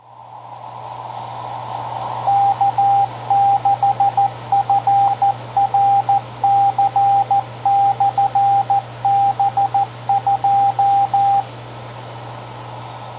Morse code, heard as a keyed tone, about 14 wpm, 800 Hz, with radio noise. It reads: K6FRC/B2
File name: tracks / K6FRCB2 921PM